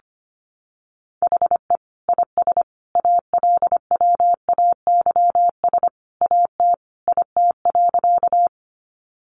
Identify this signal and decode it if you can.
Morse code, heard as a keyed tone, about 25 wpm, 700 Hz. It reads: HE IS ALWAYS AT IT.
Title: Sentences from Top 300 Words 25wpm